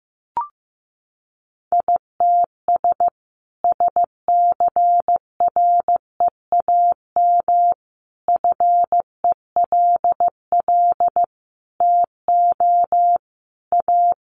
Morse code, heard as a keyed tone, about 15 wpm, 700 Hz. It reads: ITS SCREAM FELL TO A
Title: A Sound Of Thunder 15wpm